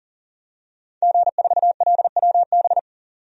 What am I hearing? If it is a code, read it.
Morse code, heard as a keyed tone, about 40 wpm, 700 Hz. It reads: G4LWB
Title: DXCC-100 Call Signs - 5 Characters - 40wpm